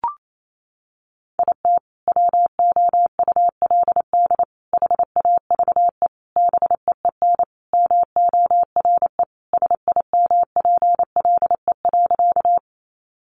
Morse code, heard as a keyed tone, about 28 wpm, 700 Hz. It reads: IT WOULD HAVE BEEN MORE SIMPLE.
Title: Sentences from Top 400 Words 28wpm